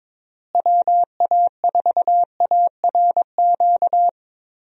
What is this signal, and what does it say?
Morse code, heard as a keyed tone, about 21 wpm, 700 Hz. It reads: WA4ARQ
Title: USA Call Signs - 6 Characters - Repeated 3x - 22wpm